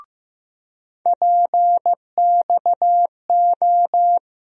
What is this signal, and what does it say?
Morse code, heard as a keyed tone, about 15 wpm, 700 Hz. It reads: PXO
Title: USA Call Sign Postfixes - 3 Letters 15wpm